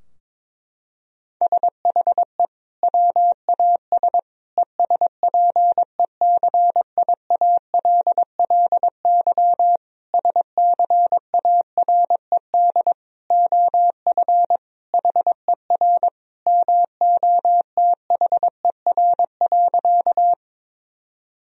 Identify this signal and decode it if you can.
Morse code, heard as a keyed tone, about 22 wpm, 700 Hz. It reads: SHE WAS ESPECIALLY SCARED OF HER MOTHER.
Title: Sentences from Top 1000 Words - Review Part 4 - 22wpm